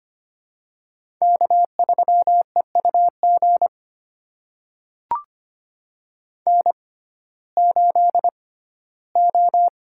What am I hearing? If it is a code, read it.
Morse code, heard as a keyed tone, about 25 wpm, spaced as for 15 wpm, 700 Hz. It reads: K3EUG N8O
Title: USA Call Signs - 5 Characters - 6x Character Spacing - 25wpm